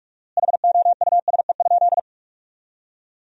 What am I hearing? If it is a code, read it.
Morse code, heard as a keyed tone, about 46 wpm, 700 Hz. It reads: HOUSE?
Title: 4x Word-Spacing - Sentences from Top 200 Words - 45wpm